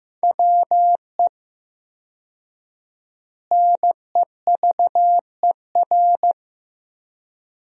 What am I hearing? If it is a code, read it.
Morse code, heard as a keyed tone, about 15 wpm, 700 Hz. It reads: WE NEVER
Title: 4x Word-Spacing - Sentences from Top 300 Words - 15wpm